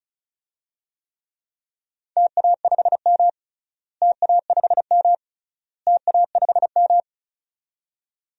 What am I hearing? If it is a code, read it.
Morse code, heard as a keyed tone, about 34 wpm, 700 Hz. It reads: TA5M TA5M TA5M
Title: DXCC-100 Call Signs - 4 Characters - Repeated 3x - 35wpm